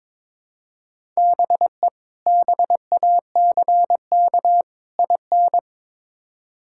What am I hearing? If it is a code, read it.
Morse code, heard as a keyed tone, about 22 wpm, 700 Hz. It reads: BE BACK IN
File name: practice / 22wpm / Build A Phrase - 5-Word Phrases - 22wpm